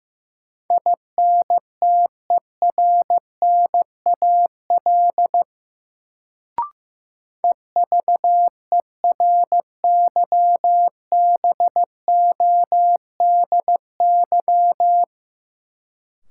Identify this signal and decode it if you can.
Morse code, heard as a keyed tone, about 15 wpm, 700 Hz. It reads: INTERNAL EVERYBODY
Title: Sentences from Top 1400 Words - New Word Drill - 15wpm